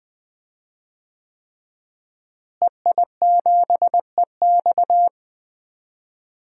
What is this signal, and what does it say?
Morse code, heard as a keyed tone, about 20 wpm, 700 Hz. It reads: EI7EX
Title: DXCC-100 Call Signs - 5 Characters - Repeated 3x - 20wpm